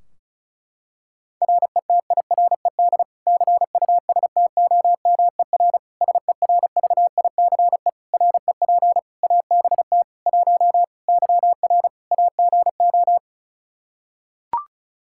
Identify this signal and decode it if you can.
Morse code, heard as a keyed tone, about 34 wpm, 700 Hz. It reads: RETIRED CUSTOMER SERVICE REP ABT 1 YR AGO
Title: Practice QSO - Exchange 3 - More Info - 35wpm